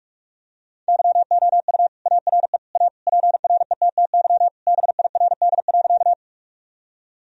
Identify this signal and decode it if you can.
Morse code, heard as a keyed tone, about 46 wpm, 700 Hz. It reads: YOU ARE A PRETTY BIRD.
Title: Sentences from Top 1000 Words - Review Part 2 - 45wpm